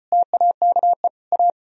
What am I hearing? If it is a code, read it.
Morse code, heard as a keyed tone, about 38 wpm, 700 Hz. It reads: TAKE A